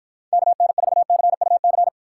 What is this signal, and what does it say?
Morse code, heard as a keyed tone, about 54 wpm, 700 Hz. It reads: KN4CUC